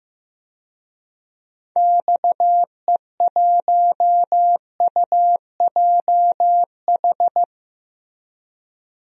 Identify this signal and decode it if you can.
Morse code, heard as a keyed tone, about 15 wpm, 700 Hz. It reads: XE1UJH
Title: DXCC-100 Call Signs - 6 Characters - Repeated 3x - 15wpm